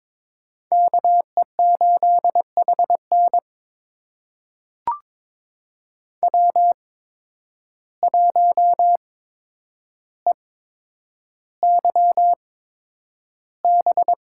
Morse code, heard as a keyed tone, about 22 wpm, spaced as for 9 wpm, 700 Hz. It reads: KE8HN W1EYB
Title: USA Call Signs - 5 Characters - 8x Character Spacing - 22wpm